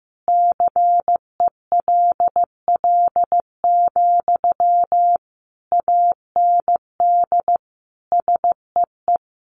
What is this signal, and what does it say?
Morse code, heard as a keyed tone, about 15 wpm, 700 Hz. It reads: CELL, AND SEE
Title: Sentences from Top 1100 Words 15wpm